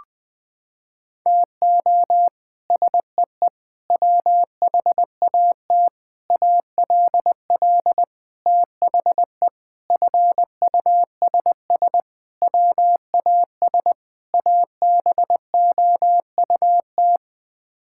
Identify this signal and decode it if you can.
Morse code, heard as a keyed tone, about 21 wpm, 700 Hz. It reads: TO SEE WHAT ALL THE FUSS WAS ABOUT
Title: Sets of 8 Words - 20wpm